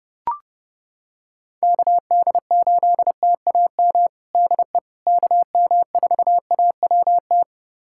Morse code, heard as a keyed tone, about 29 wpm, 700 Hz. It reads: KD8TAM DE KM4AWT
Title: USA Callsign DE Callsign - 6 Characters - 30wpm